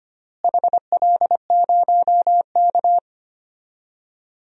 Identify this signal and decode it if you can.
Morse code, heard as a keyed tone, about 25 wpm, 700 Hz. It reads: HL0K
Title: DXCC-100 Call Signs - 4 Characters - Repeated 3x - 25wpm